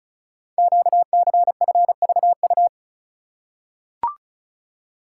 Morse code, heard as a keyed tone, about 36 wpm, 690 Hz. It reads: QCFVU
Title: Code Groups - 5 Letters - 35wpm